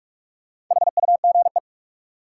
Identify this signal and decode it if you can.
Morse code, heard as a keyed tone, about 44 wpm, 700 Hz. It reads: HUGE
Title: Sentences from Top 1100 Words - New Word Drill Alt - 45wpm